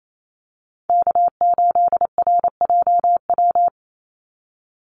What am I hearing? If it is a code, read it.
Morse code, heard as a keyed tone, about 28 wpm, 700 Hz. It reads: K8RJW
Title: Call Signs Repeated 3x - 28wpm